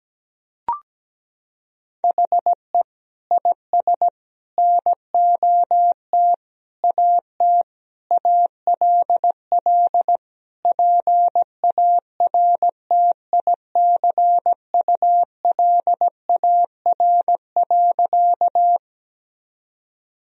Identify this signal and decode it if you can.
Morse code, heard as a keyed tone, about 17 wpm, 700 Hz. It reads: HE IS NOT AT ALL PARTICULAR.
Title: Sentences from Top 1000 Words - Review Part 4 - 17wpm